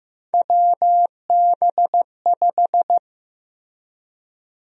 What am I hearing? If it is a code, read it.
Morse code, heard as a keyed tone, about 15 wpm, 700 Hz. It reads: WB5
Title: USA Call Sign Prefixes - Repeated 2x - 15wpm